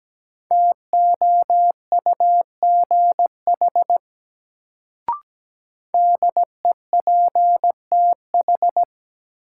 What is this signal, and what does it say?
Morse code, heard as a keyed tone, about 17 wpm, 700 Hz. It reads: TOUGH DEPTH
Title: Sentences from Top 1200 Words - New Word Drill Alt - 17wpm